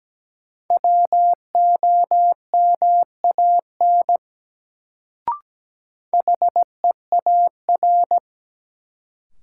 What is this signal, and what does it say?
Morse code, heard as a keyed tone, about 17 wpm, 700 Hz. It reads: WOMAN HEAR
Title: Sets of 5 Words - New Word Drill - 17wpm